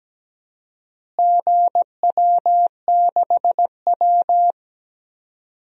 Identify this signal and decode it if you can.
Morse code, heard as a keyed tone, about 17 wpm, 700 Hz. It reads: GW6W
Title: DXCC-100 Call Signs - 4 Characters - Repeated 2x - 17wpm